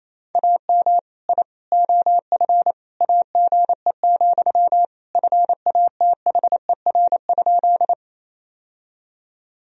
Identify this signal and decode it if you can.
Morse code, heard as a keyed tone, about 28 wpm, 700 Hz. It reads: AM I OF AGE, FATHER?
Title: Sentences from Top 1000 Words - Review Part 3 - 28wpm